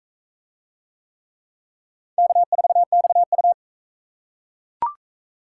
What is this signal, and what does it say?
Morse code, heard as a keyed tone, about 44 wpm, 700 Hz. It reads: K4XU